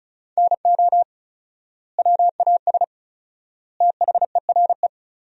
Morse code, heard as a keyed tone, about 36 wpm, 700 Hz. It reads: NO WAS THERE
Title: 4x Word-Spacing - Sentences from Top 100 Words - 35wpm